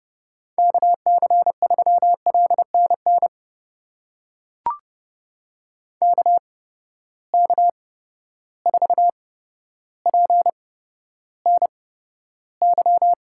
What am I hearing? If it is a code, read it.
Morse code, heard as a keyed tone, about 30 wpm, spaced as for 13 wpm, 700 Hz. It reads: KC3LNN KK4PNY
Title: USA Call Signs - 6 Characters - 8x Character Spacing - 30wpm